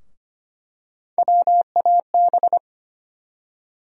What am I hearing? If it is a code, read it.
Morse code, heard as a keyed tone, about 25 wpm, 700 Hz. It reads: WAB
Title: USA Call Sign Postfixes - 3 Letters 25wpm